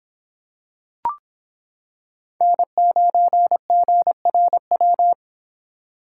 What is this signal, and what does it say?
Morse code, heard as a keyed tone, about 26 wpm, 700 Hz. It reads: N9GRW